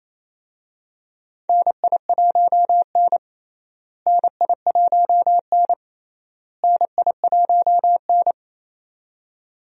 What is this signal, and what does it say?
Morse code, heard as a keyed tone, about 28 wpm, 700 Hz. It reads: NI1N NI1N NI1N
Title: USA Call Signs - 4 Characters - Repeated 3x - 28wpm